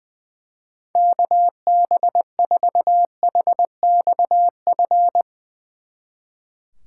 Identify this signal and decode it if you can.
Morse code, heard as a keyed tone, about 22 wpm, 700 Hz. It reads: KB4HXF
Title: Speed Racing - USA 2x3 Call Signs 20wpm